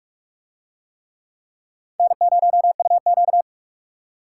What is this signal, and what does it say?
Morse code, heard as a keyed tone, about 40 wpm, 700 Hz. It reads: N0UQ